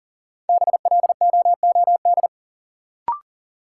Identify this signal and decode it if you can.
Morse code, heard as a keyed tone, about 40 wpm, 700 Hz. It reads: BLOOD
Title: Sentences from Top 1000 Words - Review Part 2 - New Word Drill - 40wpm